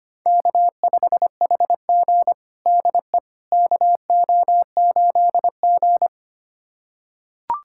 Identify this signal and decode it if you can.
Morse code, heard as a keyed tone, about 25 wpm, 700 Hz. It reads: K5HG DE KO8G
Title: USA Callsign DE Callsign - 4 Characters - 25wpm